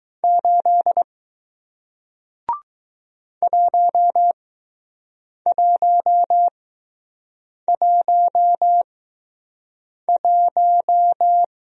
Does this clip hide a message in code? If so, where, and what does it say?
Yes: Morse code from 0.2 to 11.4 s, heard as a keyed tone, about 20 wpm, spaced as for 6 wpm, 700 Hz. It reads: 8 1111